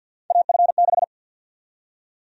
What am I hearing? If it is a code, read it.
Morse code, heard as a keyed tone, about 46 wpm, 700 Hz. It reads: AF6
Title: USA Call Sign Prefixes - 50wpm